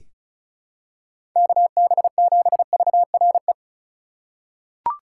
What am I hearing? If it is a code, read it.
Morse code, heard as a keyed tone, about 36 wpm, 700 Hz. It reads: KB7VRE
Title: USA Call Signs - 6 Characters - 3x Character Spacing - 35wpm